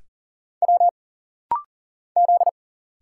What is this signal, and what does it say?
Morse code, heard as a keyed tone, about 40 wpm, 700 Hz. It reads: W Z
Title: Single Letter-Number Alt - Rapid-Fire 40wpm